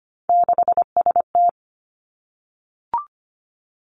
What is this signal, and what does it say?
Morse code, heard as a keyed tone, about 25 wpm, 700 Hz. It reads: 6ST
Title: Three Challenging Letter-Number Combinations 25wpm